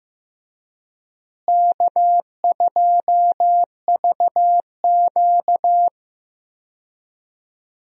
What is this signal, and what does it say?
Morse code, heard as a keyed tone, about 15 wpm, 700 Hz. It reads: K2VQ
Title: USA Call Signs - 4 Characters - Repeated 2x - 15wpm